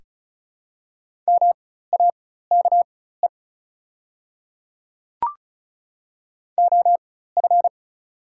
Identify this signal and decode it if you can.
Morse code, heard as a keyed tone, about 36 wpm, spaced as for 16 wpm, 690 Hz. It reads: MAKE OF
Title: 4x Character Spacing - Top 100 Words - 35wpm